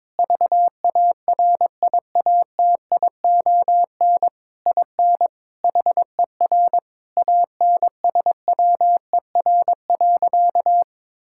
Morse code, heard as a keyed tone, about 22 wpm, 700 Hz. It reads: VARIATION IN HER ANSWER.
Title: Sentences from Top 1600 Words 22wpm